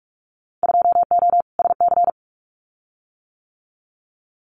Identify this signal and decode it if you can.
Morse code, heard as a keyed tone, about 44 wpm, 700 Hz. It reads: 2O SC
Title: Winter Field Day Exchanges 45wpm